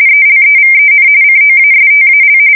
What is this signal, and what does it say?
RTTY, about 45.45 baud. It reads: ELEVEN COASTAL
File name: fsk170 / USS-New-Jersey